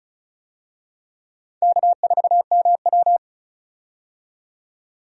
Morse code, heard as a keyed tone, about 36 wpm, 700 Hz. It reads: K4MW
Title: USA Call Signs - 4 Characters - 35wpm